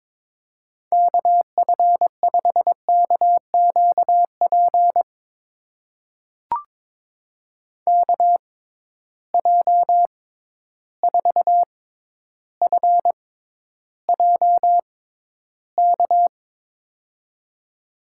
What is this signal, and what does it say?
Morse code, heard as a keyed tone, about 22 wpm, spaced as for 11 wpm, 700 Hz. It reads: KF5KQP KJ4FJK
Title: USA Call Signs - 6 Characters - 6x Character Spacing - 22wpm